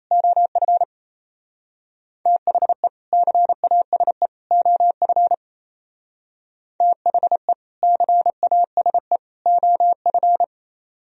Morse code, heard as a keyed tone, about 32 wpm, 700 Hz. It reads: OF THE CASE OF THE CASE OF
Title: Speed-Racing - Sets of 3 Words 28wpm